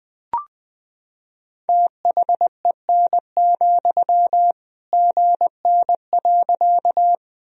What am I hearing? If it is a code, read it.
Morse code, heard as a keyed tone, about 20 wpm, 700 Hz. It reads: THEN, GN.
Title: Sentences from Top 300 Words and Abbreviations 20wpm